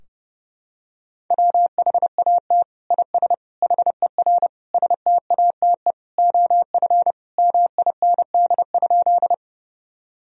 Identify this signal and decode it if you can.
Morse code, heard as a keyed tone, about 30 wpm, 700 Hz. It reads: WHAT IS HER STATE OF MIND?
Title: Sentences from Top 1000 Words - Review Part 2 - 30wpm